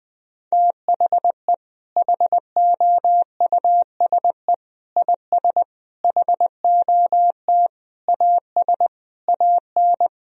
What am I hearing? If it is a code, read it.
Morse code, heard as a keyed tone, about 20 wpm, 700 Hz. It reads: THE HOUSE IS HOT AS AN
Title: Sentences from Top 1000 Words - Review Part 4 - 20wpm